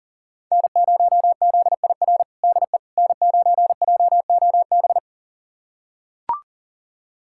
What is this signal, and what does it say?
Morse code, heard as a keyed tone, about 37 wpm, 700 Hz. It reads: N0ZIR DE N9JOB
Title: USA Callsign DE Callsign - 5 Characters - 40wpm